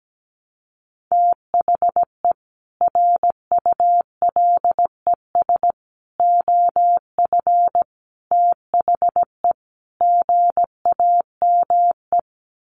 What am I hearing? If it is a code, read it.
Morse code, heard as a keyed tone, about 17 wpm, 700 Hz. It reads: THE RULES OF THE GAME
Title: Sets of 5 Words Encore2 17wpm